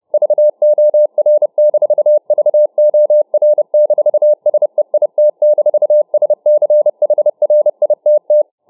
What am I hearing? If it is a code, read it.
Morse code, heard as a keyed tone, about 29 wpm, 600 Hz. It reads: VOR-VOR-SEIT-SCHRITT